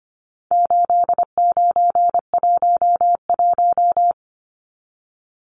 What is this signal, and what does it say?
Morse code, heard as a keyed tone, about 25 wpm, 700 Hz. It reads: 8911
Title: Numbers - 4-Digit 25wpm